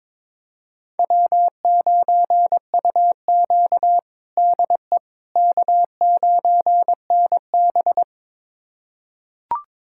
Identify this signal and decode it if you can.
Morse code, heard as a keyed tone, about 22 wpm, 700 Hz. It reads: W9UQ DE K9NB
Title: USA Callsign DE Callsign - 4 Characters - 22wpm